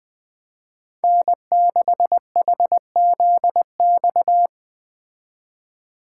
Morse code, heard as a keyed tone, about 20 wpm, 700 Hz. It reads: N6HZX
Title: Speed Racing - USA 1x3 Call Signs 17wpm